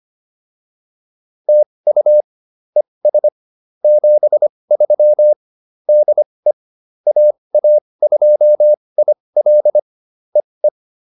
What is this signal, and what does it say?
Morse code, heard as a keyed tone, about 25 wpm, spaced as for 20 wpm, 600 Hz. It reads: TU ES 73 DE AA2IL EE